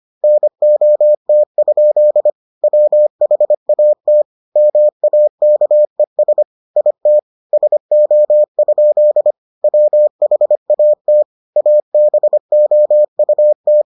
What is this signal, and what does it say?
Morse code, heard as a keyed tone, about 25 wpm, 600 Hz. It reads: NOT? WHAT MAKES IT SO? WHAT ABOUT